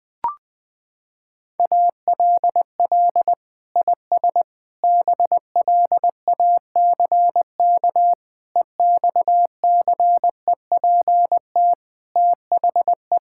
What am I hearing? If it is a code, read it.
Morse code, heard as a keyed tone, about 20 wpm, 700 Hz. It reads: ALL IS BLACK EXCEPT THE
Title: Sentences from Top 1600 Words 20wpm